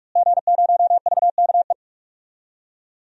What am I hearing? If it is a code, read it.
Morse code, heard as a keyed tone, about 43 wpm, 700 Hz. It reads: G0VKE
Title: DXCC-100 Call Signs - 5 Characters - Repeated 3x - 45wpm